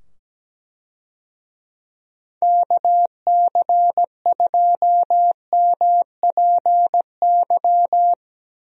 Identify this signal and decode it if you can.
Morse code, heard as a keyed tone, about 17 wpm, 700 Hz. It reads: KC2MPY
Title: USA Call Signs - 6 Characters - 17wpm